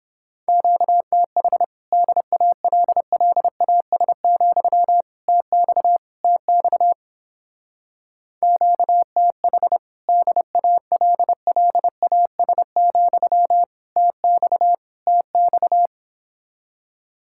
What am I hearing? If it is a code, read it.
Morse code, heard as a keyed tone, about 29 wpm, 700 Hz. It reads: QTH DALLAS, TX TX QTH DALLAS, TX TX